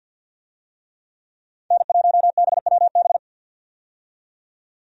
Morse code, heard as a keyed tone, about 50 wpm, 700 Hz. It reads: N1BWB
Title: USA Call Signs - 5 Characters - 50wpm